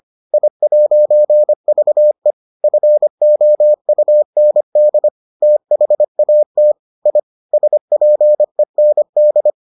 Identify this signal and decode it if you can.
Morse code, heard as a keyed tone, about 25 wpm, 600 Hz. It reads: I'VE FOUND THAT I SPEND